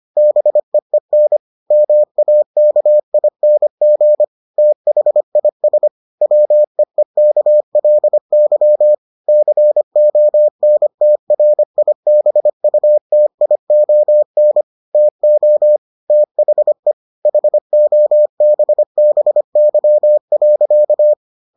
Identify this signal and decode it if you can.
Morse code, heard as a keyed tone, about 25 wpm, 600 Hz. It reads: BEEN MAKING THIS WEEKLY CONTRIBUTION TO THE HOBBY.